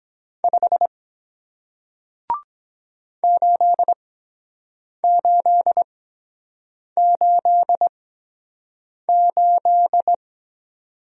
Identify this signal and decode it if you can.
Morse code, heard as a keyed tone, about 23 wpm, spaced as for 7 wpm, 700 Hz. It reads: 5 8888